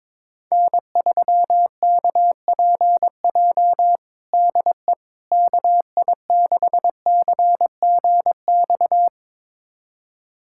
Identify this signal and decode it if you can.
Morse code, heard as a keyed tone, about 22 wpm, 700 Hz. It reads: N3KPJ DE KI6CGX
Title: USA Callsign DE Callsign 22wpm